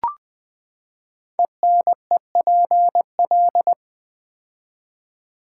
Morse code, heard as a keyed tone, about 20 wpm, 700 Hz. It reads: ENEPL
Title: Code Groups - 5 Letters - 20wpm